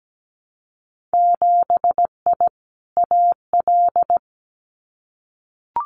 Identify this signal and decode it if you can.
Morse code, heard as a keyed tone, about 17 wpm, 700 Hz. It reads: 7I AL
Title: Winter Field Day Exchanges Encore 17wpm